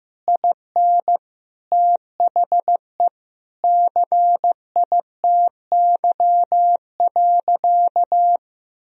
Morse code, heard as a keyed tone, about 15 wpm, 700 Hz. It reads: IN THE CITY.
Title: Sentences from Top 1000 Words - Review Part 1 - 15wpm